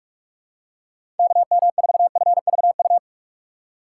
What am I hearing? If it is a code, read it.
Morse code, heard as a keyed tone, about 46 wpm, 700 Hz. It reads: KM4FVU
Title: USA Call Signs - 6 Characters - 2x Character Spacing - 45wpm